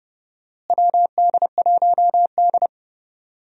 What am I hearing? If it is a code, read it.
Morse code, heard as a keyed tone, about 32 wpm, 700 Hz. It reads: WD1D
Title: USA Call Signs - 4 Characters - Repeated 2x - 30wpm